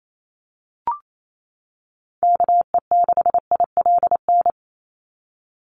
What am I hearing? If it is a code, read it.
Morse code, heard as a keyed tone, about 28 wpm, 700 Hz. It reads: KE6ILN
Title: USA Call Signs 28wpm